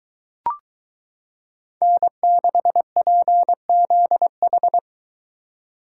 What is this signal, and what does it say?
Morse code, heard as a keyed tone, about 22 wpm, 700 Hz. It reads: N6PZH